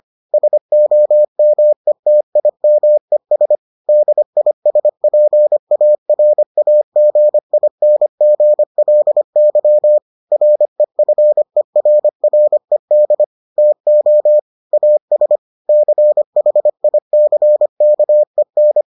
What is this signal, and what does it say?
Morse code, heard as a keyed tone, about 25 wpm, 600 Hz. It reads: SOMETIMES DISPARAGINGLY REFERRED TO AS CHICKEN